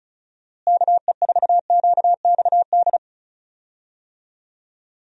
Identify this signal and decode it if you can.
Morse code, heard as a keyed tone, about 36 wpm, 700 Hz. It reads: KE4QXD
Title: USA Call Signs - 6 Characters - Repeated 3x - 35wpm